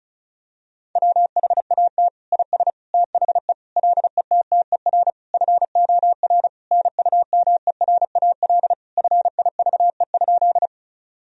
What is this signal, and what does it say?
Morse code, heard as a keyed tone, about 36 wpm, 700 Hz. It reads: WHAT IS THE LETTER FOR NUMERAL FIVE?
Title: Sentences from Top 1000 Words - Review Part 2 - 35wpm